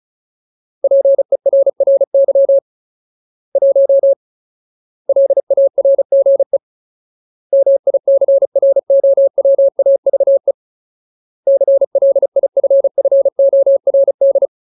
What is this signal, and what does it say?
Morse code, heard as a keyed tone, about 34 wpm, 550 Hz. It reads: PERRY 1 LARGE MICROWAVE CLIFFORD
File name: Ultimate Word List / 35WPM Ultimate